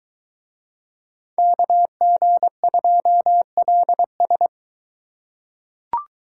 Morse code, heard as a keyed tone, about 22 wpm, 700 Hz. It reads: KG2LS